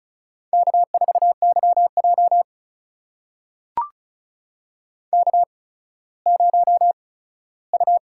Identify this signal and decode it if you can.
Morse code, heard as a keyed tone, about 36 wpm, spaced as for 17 wpm, 700 Hz. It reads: K4YJ K0U
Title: USA Call Signs - 4 Characters - 8x Character Spacing - 35wpm